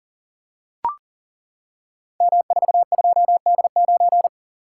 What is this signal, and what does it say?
Morse code, heard as a keyed tone, about 39 wpm, 690 Hz. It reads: M42D9
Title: Code Groups - 5 Letter:Numbers - 40wpm